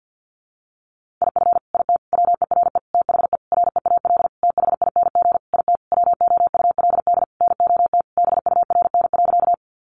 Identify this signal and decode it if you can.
Morse code, heard as a keyed tone, about 50 wpm, 700 Hz. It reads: IF IT WERE THE REAL THING IT WOULD NOT BURN.
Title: Sentences from Top 1300 Words 50wpm